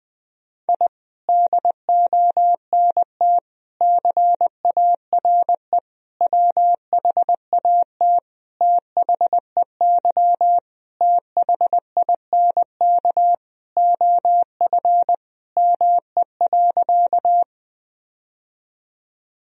Morse code, heard as a keyed tone, about 20 wpm, 700 Hz. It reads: I DONT CARE WHAT THEY THINK OF ME.
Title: Sentences from Top 1000 Words - Review Part 1 - 20wpm